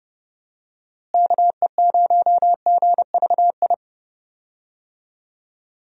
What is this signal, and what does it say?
Morse code, heard as a keyed tone, about 30 wpm, 700 Hz. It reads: KE0GVI
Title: USA Call Signs - 6 Characters - Repeated 2x - 30wpm